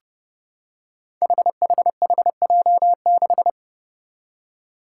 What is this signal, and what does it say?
Morse code, heard as a keyed tone, about 32 wpm, 700 Hz. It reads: HHHJ6
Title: Five Challenging Letter-Number Combinations 30wpm